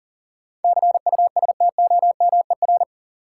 Morse code, heard as a keyed tone, about 40 wpm, 700 Hz. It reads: CUSTOMER
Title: Sentences from Top 1100 Words - New Word Drill - 40wpm